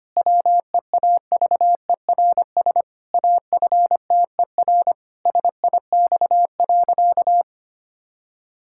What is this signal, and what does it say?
Morse code, heard as a keyed tone, about 25 wpm, 700 Hz. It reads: WEAVERS AFTER SIX.
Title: Pangram Challenge v2 - 25wpm